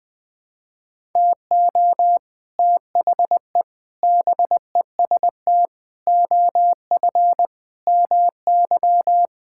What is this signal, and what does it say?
Morse code, heard as a keyed tone, about 20 wpm, 700 Hz. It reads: TO THE BEST OF MY
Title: Sentences with 6-Word Phrases - Spoken Beginning - 20wpm